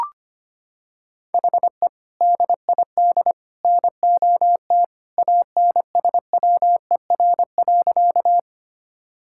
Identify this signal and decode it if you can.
Morse code, heard as a keyed tone, about 25 wpm, 700 Hz. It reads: HE DID NOT ANSWER.
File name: practice / 25wpm / Sentences with 3-Word Phrases - 25wpm